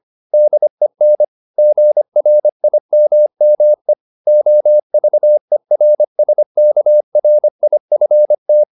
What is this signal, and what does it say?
Morse code, heard as a keyed tone, about 25 wpm, 600 Hz. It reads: DEN GRIMME OVERSKRIFT
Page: Ferie 25 wpm